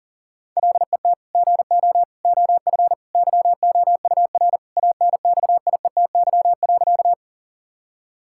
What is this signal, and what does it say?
Morse code, heard as a keyed tone, about 40 wpm, 700 Hz. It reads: LET GO OF YOUR ANXIETY.
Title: Sentences from Top 1600 Words 40wpm